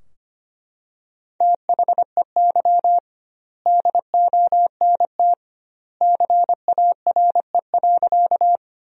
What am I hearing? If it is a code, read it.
Morse code, heard as a keyed tone, about 25 wpm, 700 Hz. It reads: THEY DONT CARE.
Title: 2x Word-Spacing - Sentences from Top 300 Words - 25wpm